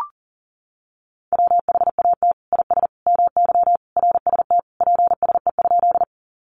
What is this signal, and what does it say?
Morse code, heard as a keyed tone, about 40 wpm, 700 Hz. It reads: WHAT IS MY RST PSE?
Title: Common QSO Phrases 40wpm